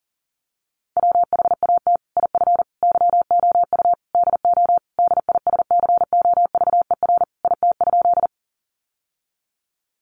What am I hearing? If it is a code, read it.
Morse code, heard as a keyed tone, about 40 wpm, 700 Hz. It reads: WHAT IF YOU DO DISCOVER IT?
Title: Sentences from Top 1100 Words 40wpm